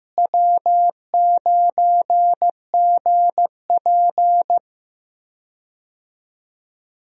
Morse code, heard as a keyed tone, about 15 wpm, 700 Hz. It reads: W9GP
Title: USA Call Signs - 4 Characters - Repeated 2x - 15wpm